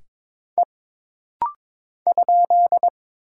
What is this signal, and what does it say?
Morse code, heard as a keyed tone, about 22 wpm, 700 Hz. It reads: E ?
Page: This rapid-fire timing is geared to push your ICR (Instant Character Recognition) skills.